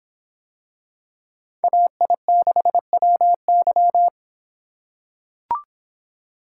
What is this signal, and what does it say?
Morse code, heard as a keyed tone, about 25 wpm, 700 Hz. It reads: AI6WY